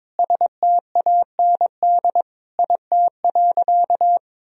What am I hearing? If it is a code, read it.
Morse code, heard as a keyed tone, about 22 wpm, 700 Hz. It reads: STAND IT.
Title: Sentences from Top 1000 Words - Review Part 1 - 22wpm